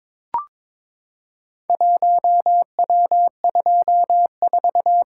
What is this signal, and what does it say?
Morse code, heard as a keyed tone, about 23 wpm, 690 Hz. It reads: 1W24
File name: practice / 22wpm / Four Letter-Number Combinations - 22wpm